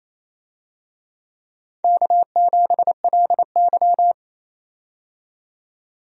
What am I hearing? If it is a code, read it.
Morse code, heard as a keyed tone, about 28 wpm, 700 Hz. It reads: K7LY
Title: USA Call Signs - 4 Characters - 28wpm